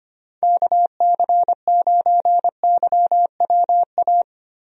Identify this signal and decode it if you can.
Morse code, heard as a keyed tone, about 25 wpm, 700 Hz. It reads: KC9YWA
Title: USA Call Signs - 6 Characters - Repeated 2x - 25wpm